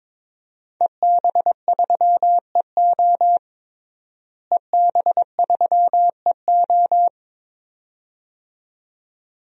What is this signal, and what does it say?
Morse code, heard as a keyed tone, about 22 wpm, 700 Hz. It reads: EB3EO EB3EO
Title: DXCC-100 Call Signs - 5 Characters - Repeated 2x - 22wpm